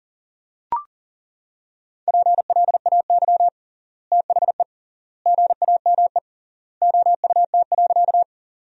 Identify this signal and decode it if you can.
Morse code, heard as a keyed tone, about 37 wpm, 700 Hz. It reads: PLAY THE GAME OUT.
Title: 3x Word-Spacing - Sentences from Top 500 Words - 40wpm